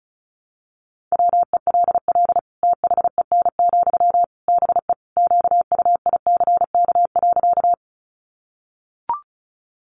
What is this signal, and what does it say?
Morse code, heard as a keyed tone, about 34 wpm, 700 Hz. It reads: WELL THEN, BE QUICK.
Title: Sentences from Top 500 Words 35wpm